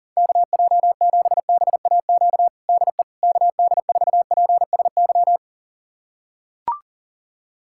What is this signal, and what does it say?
Morse code, heard as a keyed tone, about 40 wpm, 700 Hz. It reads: KJ7BAQ DE KD4PSY
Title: USA Callsign DE Callsign 40wpm